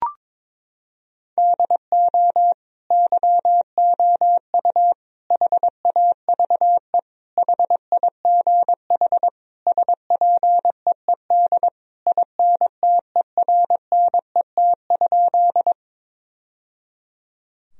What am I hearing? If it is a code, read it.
Morse code, heard as a keyed tone, about 22 wpm, 700 Hz. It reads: DO YOU HAVE HIGH SPEED INTERNET?
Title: Sentences from Top 1000 Words - Review Part 4 - 22wpm